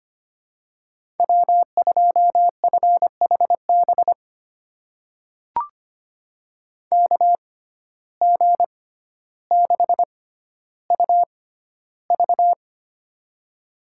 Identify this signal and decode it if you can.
Morse code, heard as a keyed tone, about 25 wpm, spaced as for 13 wpm, 700 Hz. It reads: W2FHB KG6UV
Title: USA Call Signs - 5 Characters - 6x Character Spacing - 25wpm